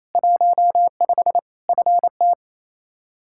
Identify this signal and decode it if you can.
Morse code, heard as a keyed tone, about 28 wpm, 700 Hz. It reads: 15 FT
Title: K1USN SST QSO Practice 28wpm